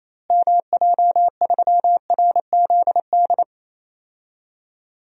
Morse code, heard as a keyed tone, about 28 wpm, 700 Hz. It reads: MJ3RZD
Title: DXCC-100 Call Signs - 6 Characters - Repeated 3x - 28wpm